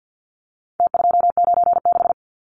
Morse code, heard as a keyed tone, about 50 wpm, 700 Hz. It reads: T296
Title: Numbers - Contest Style - 4-Digit 50wpm